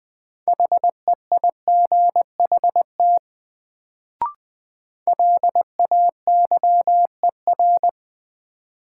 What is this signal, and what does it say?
Morse code, heard as a keyed tone, about 20 wpm, 700 Hz. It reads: HEIGHT LAYER
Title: Sentences from Top 1600 Words - New Word Drill - 20wpm